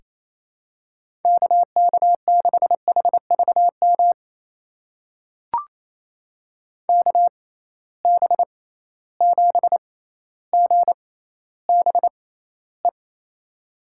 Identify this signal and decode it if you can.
Morse code, heard as a keyed tone, about 28 wpm, spaced as for 14 wpm, 700 Hz. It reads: KK6HVM KB7GBE
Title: USA Call Signs - 6 Characters - 6x Character Spacing - 28wpm